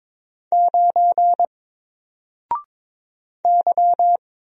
Morse code, heard as a keyed tone, about 22 wpm, 700 Hz. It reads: 9 Y
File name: Single-Letter-Number-22wpm